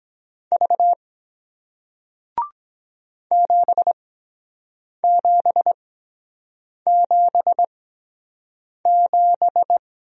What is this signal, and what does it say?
Morse code, heard as a keyed tone, about 22 wpm, spaced as for 7 wpm, 700 Hz. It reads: V 7777